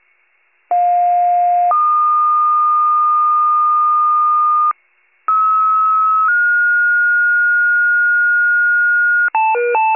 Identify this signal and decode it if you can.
Two-tone page, heard as a one-second tone, then a three-second tone, about 710 then 1220 Hz, about 1290 then 1430 Hz. It is heard from 0.7 to 9.3 s.